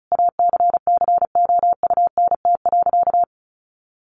Morse code, heard as a keyed tone, about 34 wpm, 700 Hz. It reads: ACCOUNT.
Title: Sentences from Top 900 Words 35wpm